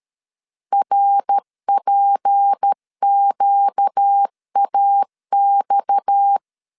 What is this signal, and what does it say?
Morse code, heard as a keyed tone, about 13 wpm, 800 Hz. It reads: RPQAX
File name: CW / DET / 1 6 1500C S A 12wpm 3440